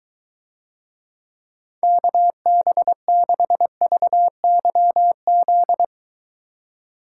Morse code, heard as a keyed tone, about 23 wpm, 700 Hz. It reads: KB6VYZ